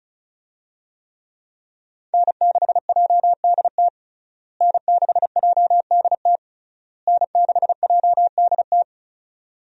Morse code, heard as a keyed tone, about 35 wpm, 700 Hz. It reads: N6JDT N6JDT N6JDT
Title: USA Call Signs - 5 Characters - Repeated 3x - 35wpm